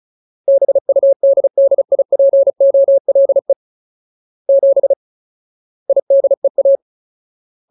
Morse code, heard as a keyed tone, about 35 wpm, 550 Hz. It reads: BUDDIPOLE 7 IDEA
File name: Ultimate Word List / 35WPM Ultimate